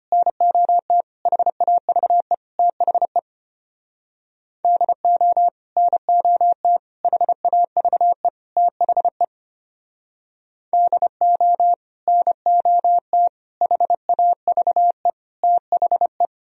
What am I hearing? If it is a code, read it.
Morse code, heard as a keyed tone, about 32 wpm, 700 Hz. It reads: NOT HAVE THE DO NOT HAVE THE DO NOT HAVE THE